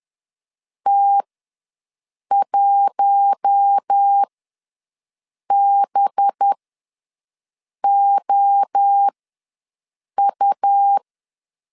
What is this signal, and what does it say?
Morse code, heard as a keyed tone, about 10 wpm, spaced as for 5 wpm, 800 Hz. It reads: T1BOU
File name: LEZIONI / 6 PKX0 / 6 PKX0 ws 10wpm 4220